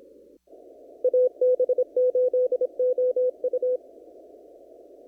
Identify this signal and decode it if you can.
Morse code, heard as a keyed tone, about 26 wpm, 500 Hz. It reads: AB8OU
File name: CW / kt5j wpx cw-op-k5pi